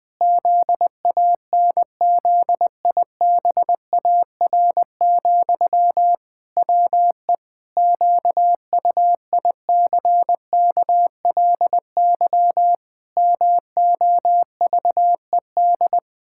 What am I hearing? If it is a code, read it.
Morse code, heard as a keyed tone, about 20 wpm, 700 Hz. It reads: ZANZIBAR, WE QUICKLY MOVED
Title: Pangram Challenge v2 - 20wpm